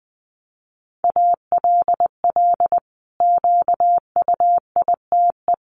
Morse code, heard as a keyed tone, about 20 wpm, 700 Hz. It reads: ALL QUITE
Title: Sentences from Top 700 Words 20wpm